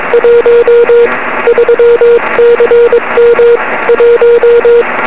Morse code, heard as a keyed tone, about 22 wpm, 470 Hz, with radio noise. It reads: 1 3CM 1
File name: 2111z pa2m 23cm